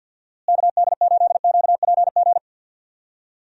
Morse code, heard as a keyed tone, about 47 wpm, 700 Hz. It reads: KD8QPG
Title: USA Call Signs - 6 Characters - Repeated 3x - 50wpm